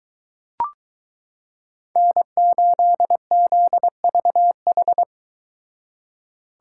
Morse code, heard as a keyed tone, about 22 wpm, 700 Hz. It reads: N8ZVH